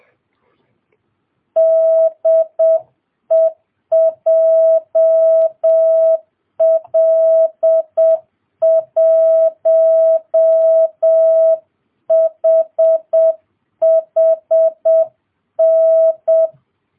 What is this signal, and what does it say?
Morse code, heard as a keyed tone, about 7 wpm, 650 Hz. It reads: DE JL1HHN